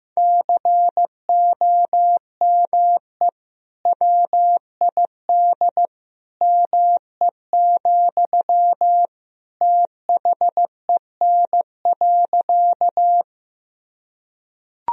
Morse code, heard as a keyed tone, about 15 wpm, 700 Hz. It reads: COME WID ME, THEN.
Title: Sentences from Top 200 Words and Abbreviations 15wpm